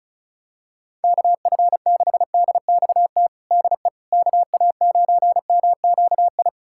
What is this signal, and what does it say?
Morse code, heard as a keyed tone, about 34 wpm, 700 Hz. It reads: KF6DXT DE KA9MQI
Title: USA Callsign DE Callsign - 6 Characters - 35wpm